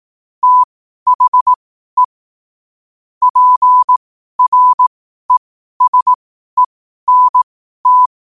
Morse code, heard as a keyed tone, about 18 wpm, spaced as for 12 wpm, 1000 Hz. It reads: THE PRESENT
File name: qotdfiles / 12wpm